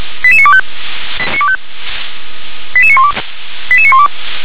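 Five-tone selective call (ZVEI), heard as quick runs of five tones, tones of about 70 ms. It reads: 88925 88912 88912
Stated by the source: ZVEI five-tone call received by SDR.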